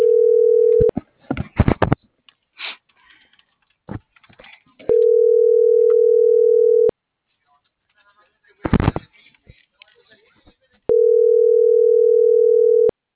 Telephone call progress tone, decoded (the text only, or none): ringback tone